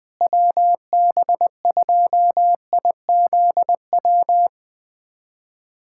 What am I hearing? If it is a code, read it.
Morse code, heard as a keyed tone, about 21 wpm, 700 Hz. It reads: WB2IZW
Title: USA Callsign DE Callsign 20wpm